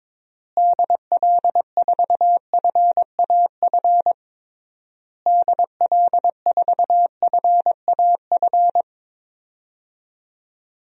Morse code, heard as a keyed tone, about 21 wpm, 700 Hz. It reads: DL4FAF DL4FAF
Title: DXCC-100 Call Signs - 6 Characters - Repeated 3x - 22wpm